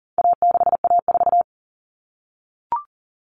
Morse code, heard as a keyed tone, about 40 wpm, 700 Hz. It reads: A6A4
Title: Common Cut Numbers - 4-Digit 40wpm